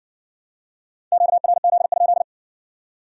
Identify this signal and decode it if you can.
Morse code, heard as a keyed tone, about 54 wpm, 700 Hz. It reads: QRZ?